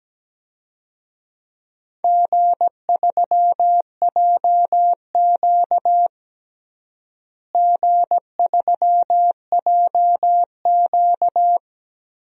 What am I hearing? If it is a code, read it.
Morse code, heard as a keyed tone, about 17 wpm, 700 Hz. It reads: G3JQ G3JQ
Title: DXCC-100 Call Signs - 4 Characters - Repeated 3x - 17wpm